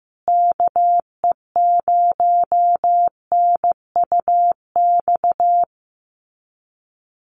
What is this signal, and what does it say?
Morse code, heard as a keyed tone, about 15 wpm, 700 Hz. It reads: KE0NUX
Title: USA Call Signs Repeated 3x 15wpm